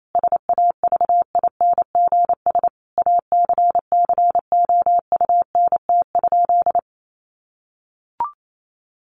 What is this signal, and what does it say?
Morse code, heard as a keyed tone, about 28 wpm, 700 Hz. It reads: SAVINGS ACCOUNT?
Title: Sentences from Top 1500 Words 28wpm